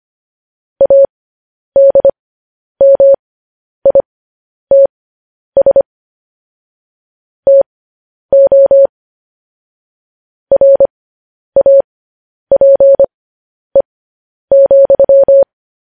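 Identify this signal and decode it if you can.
Morse code, heard as a keyed tone, about 25 wpm, spaced as for 10 wpm, 550 Hz. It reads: ADMITS TO RAPE,
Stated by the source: Morse code news headlines sent at 10 wpm.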